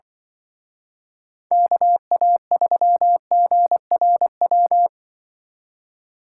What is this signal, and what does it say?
Morse code, heard as a keyed tone, about 23 wpm, 700 Hz. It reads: KA3GRW
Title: Speed Racing - USA 2x3 Call Signs 20wpm